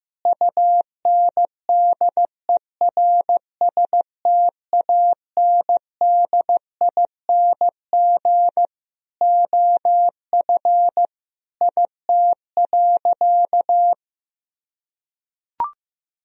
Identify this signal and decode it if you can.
Morse code, heard as a keyed tone, about 15 wpm, 700 Hz. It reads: UNDERSTANDING OF IT.
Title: Sentences from Top 1000 Words - Review Part 3 - 15wpm